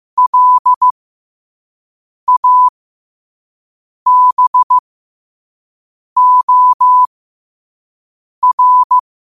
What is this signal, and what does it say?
Morse code, heard as a keyed tone, about 15 wpm, spaced as for 5 wpm, 1000 Hz. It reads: LABOR
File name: qotdfiles / 5wpm